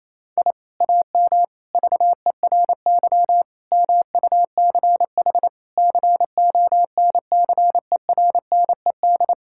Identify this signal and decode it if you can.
Morse code, heard as a keyed tone, about 28 wpm, 700 Hz. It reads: I AM VERY MUCH CONCERNED
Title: Sentences from Top 1000 Words - Review Part 6 - 28wpm